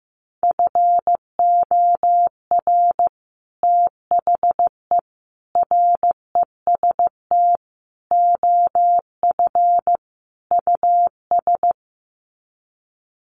Morse code, heard as a keyed tone, about 15 wpm, 700 Hz. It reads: FOR THE REST OF US
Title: Sets of 5 Words Encore 15wpm